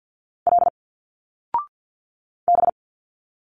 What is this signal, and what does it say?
Morse code, heard as a keyed tone, about 50 wpm, 700 Hz. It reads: L B
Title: Single Letters 50wpm